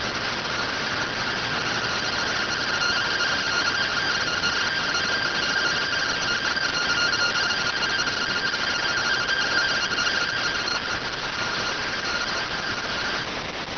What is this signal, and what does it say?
Two-tone teleprinter signal, about 170 Hz shift.